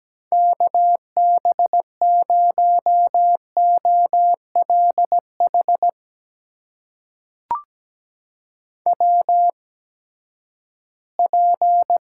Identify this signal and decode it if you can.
Morse code, heard as a keyed tone, about 17 wpm, spaced as for 12 wpm, 700 Hz. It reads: KB0OLH WP
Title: USA Call Signs - 6 Characters - 8x Character Spacing - 17wpm